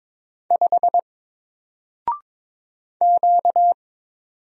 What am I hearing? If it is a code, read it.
Morse code, heard as a keyed tone, about 22 wpm, 700 Hz. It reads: 5 Q
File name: Single-Letter-Number-22wpm